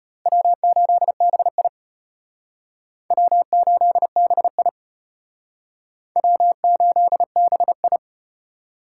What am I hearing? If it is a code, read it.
Morse code, heard as a keyed tone, about 34 wpm, 700 Hz. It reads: W8BI W8BI W8BI